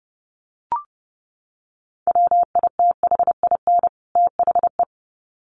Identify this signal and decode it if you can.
Morse code, heard as a keyed tone, about 30 wpm, 700 Hz. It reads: WITHIN THE
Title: Sets of 2 Words 30wpm